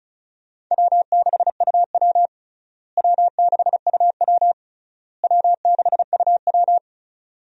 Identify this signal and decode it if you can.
Morse code, heard as a keyed tone, about 36 wpm, 700 Hz. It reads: W6UW W6UW W6UW
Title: USA Call Signs - 4 Characters - Repeated 3x - 35wpm